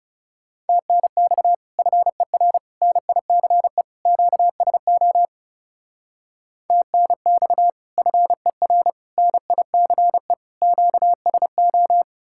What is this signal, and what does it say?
Morse code, heard as a keyed tone, about 37 wpm, 700 Hz. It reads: TNX FER NICE QSO TNX FER NICE QSO
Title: Speed-Racing Common QSO Phrases - 30wpm